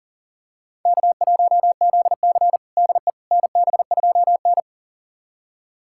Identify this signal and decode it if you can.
Morse code, heard as a keyed tone, about 40 wpm, 700 Hz. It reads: K1ZC DE NB2N
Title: USA Callsign DE Callsign - 4 Characters - 40wpm